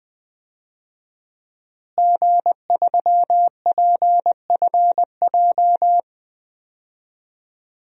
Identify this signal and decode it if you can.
Morse code, heard as a keyed tone, about 20 wpm, 700 Hz. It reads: G3PFJ
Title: DXCC-100 Call Signs - 5 Characters - Repeated 2x - 20wpm